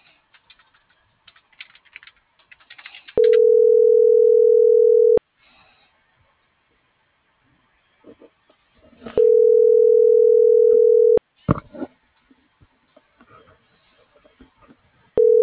Ringback tone (North America), 440 + 480 Hz.